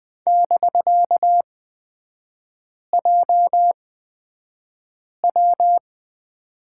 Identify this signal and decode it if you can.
Morse code, heard as a keyed tone, about 20 wpm, spaced as for 5 wpm, 700 Hz. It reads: <BK>JW
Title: 250 Lesson 40e - BK - Three Characters 20wpm